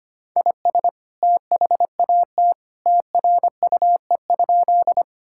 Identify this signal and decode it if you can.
Morse code, heard as a keyed tone, about 25 wpm, 700 Hz. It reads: IS THAT TRUE?
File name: practice / 25wpm / Sentences with 2-Word Phrases - 25wpm